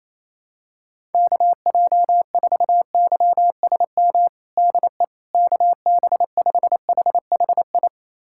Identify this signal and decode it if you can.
Morse code, heard as a keyed tone, about 28 wpm, 700 Hz. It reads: KJ4YSM DE KB5HHI
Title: USA Callsign DE Callsign - 6 Characters - 28wpm